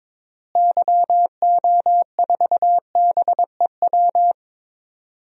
Morse code, heard as a keyed tone, about 22 wpm, 700 Hz. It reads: YO4BEW
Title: DXCC-100 Call Signs - 6 Characters - Repeated 3x - 22wpm